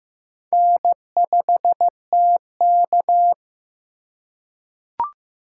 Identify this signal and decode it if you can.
Morse code, heard as a keyed tone, about 15 wpm, 700 Hz. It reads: N5TK
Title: USA Call Signs - 4 Characters - 2x Character Spacing - 15wpm